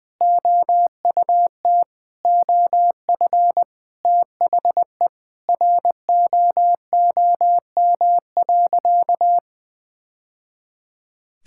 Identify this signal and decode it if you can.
Morse code, heard as a keyed tone, about 21 wpm, 700 Hz. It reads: OUT OF THE ROOM.
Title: Sentences with 3-Word Phrases - 20wpm